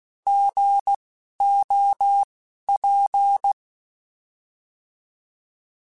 Morse code, heard as a keyed tone, about 16 wpm, spaced as for 11 wpm, 800 Hz. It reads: GOP
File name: Alphabetical / Alphabetical mp3s / Gs @10